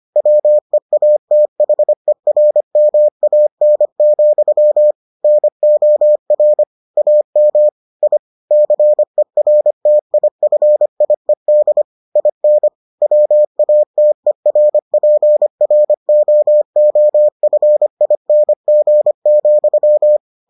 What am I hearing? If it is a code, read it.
Morse code, heard as a keyed tone, about 25 wpm, 600 Hz. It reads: WEATHERMAN, NOR AM I CERTIFIED IN WATERPROOFING,